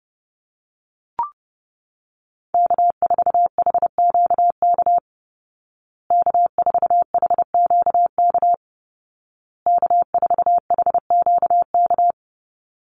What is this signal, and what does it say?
Morse code, heard as a keyed tone, about 30 wpm, 700 Hz. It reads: K4HQK K4HQK K4HQK
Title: Call Signs Repeated 2x - 30wpm